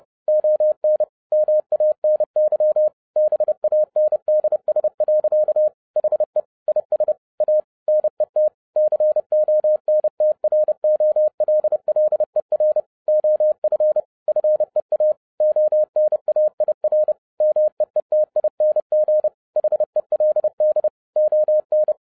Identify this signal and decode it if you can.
Morse code, heard as a keyed tone, about 30 wpm, 610 Hz. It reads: ON MANY BANDS. HE IS A NET CONTROLLER OF FEA ONAIR MEETING HELD ON